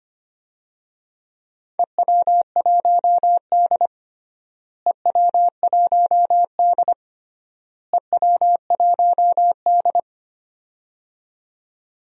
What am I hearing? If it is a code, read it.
Morse code, heard as a keyed tone, about 25 wpm, 700 Hz. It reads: EW1D EW1D EW1D
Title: DXCC-100 Call Signs - 4 Characters - Repeated 3x - 25wpm